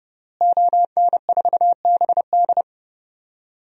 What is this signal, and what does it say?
Morse code, heard as a keyed tone, about 30 wpm, 700 Hz. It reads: ON4BD
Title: DXCC-100 Call Signs - 5 Characters - Repeated 2x - 30wpm